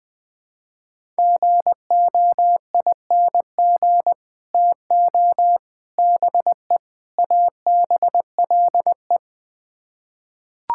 Morse code, heard as a keyed tone, about 21 wpm, 700 Hz. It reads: GOING TO BE ABLE
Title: Build A Phrase - 4-Word Phrases - 20wpm